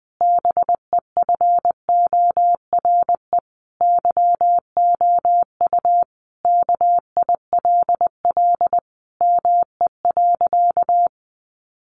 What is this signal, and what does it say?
Morse code, heard as a keyed tone, about 20 wpm, 700 Hz. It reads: BEFORE YOU KILL ME.
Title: Sentences from Top 1400 Words 20wpm